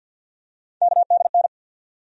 Morse code, heard as a keyed tone, about 50 wpm, 700 Hz. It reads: KDN
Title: USA Call Sign Postfixes - 3 Letters 50wpm